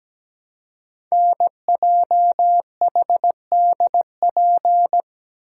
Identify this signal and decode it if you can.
Morse code, heard as a keyed tone, about 17 wpm, 700 Hz. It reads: NJHDP
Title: Code Groups - 5 Letters - 17wpm